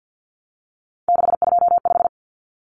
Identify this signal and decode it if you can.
Morse code, heard as a keyed tone, about 50 wpm, 700 Hz. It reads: 625
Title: Numbers - 3-Digit 50wpm